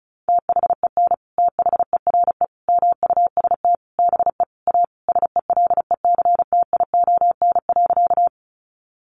Morse code, heard as a keyed tone, about 34 wpm, 700 Hz. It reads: THEN THERE MUST BE A SELECTION.
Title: Sentences from Top 1400 Words 35wpm